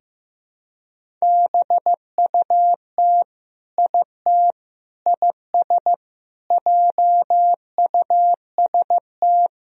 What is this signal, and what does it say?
Morse code, heard as a keyed tone, about 15 wpm, 700 Hz. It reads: BUT IT IS JUST
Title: Sentences from Top 200 Words and Abbreviations 15wpm